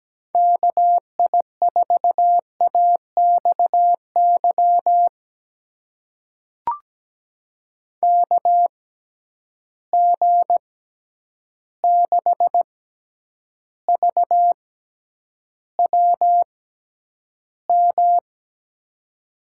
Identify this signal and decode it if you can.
Morse code, heard as a keyed tone, about 17 wpm, spaced as for 9 wpm, 700 Hz. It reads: KI4AXY KG6VWM
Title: USA Call Signs - 6 Characters - 6x Character Spacing - 17wpm